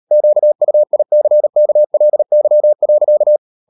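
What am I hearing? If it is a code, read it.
Morse code, heard as a keyed tone, about 38 wpm, 600 Hz. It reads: QUICKLY.
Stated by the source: Morse code practice at 35–39 wpm sending English text of Ken Follett's novel Never, from CW med Gnister.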